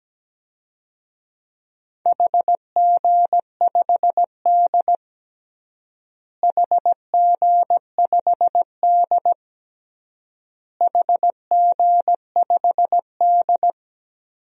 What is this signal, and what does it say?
Morse code, heard as a keyed tone, about 17 wpm, 700 Hz. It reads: HG5D HG5D HG5D
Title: DXCC-100 Call Signs - 4 Characters - Repeated 3x - 17wpm